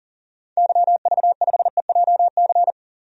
Morse code, heard as a keyed tone, about 40 wpm, 700 Hz. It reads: YV5EJC
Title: DXCC-100 Call Signs - 6 Characters - 40wpm